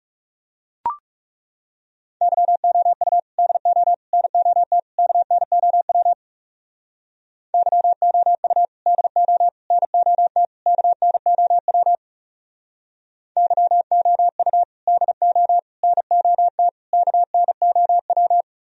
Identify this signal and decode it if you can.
Morse code, heard as a keyed tone, about 38 wpm, 700 Hz. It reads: YOU DO NOT KNOW YOU DO NOT KNOW YOU DO NOT KNOW